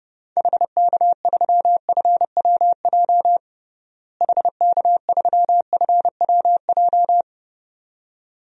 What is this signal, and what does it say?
Morse code, heard as a keyed tone, about 32 wpm, 700 Hz. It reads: HK3FWJ HK3FWJ
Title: DXCC-100 Call Signs - 6 Characters - Repeated 2x - 30wpm